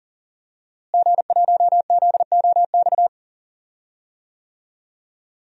Morse code, heard as a keyed tone, about 40 wpm, 700 Hz. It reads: G1ZOX
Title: DXCC-100 Call Signs - 5 Characters - Repeated 3x - 40wpm